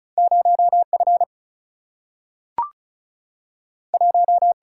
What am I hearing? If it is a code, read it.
Morse code, heard as a keyed tone, about 36 wpm, 700 Hz. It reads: 0 F 1
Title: Single-Letter Single-Number 35wpm